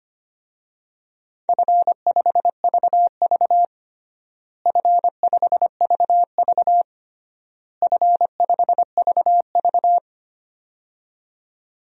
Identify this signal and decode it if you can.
Morse code, heard as a keyed tone, about 25 wpm, 700 Hz. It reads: F5VV F5VV F5VV
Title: DXCC-100 Call Signs - 4 Characters - Repeated 3x - 25wpm